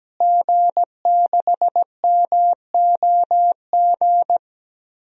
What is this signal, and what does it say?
Morse code, heard as a keyed tone, about 17 wpm, 700 Hz. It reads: G6MOG
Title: DXCC-100 Call Signs - 5 Characters - Repeated 3x - 17wpm